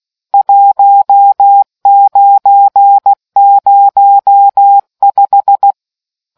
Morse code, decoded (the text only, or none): 1905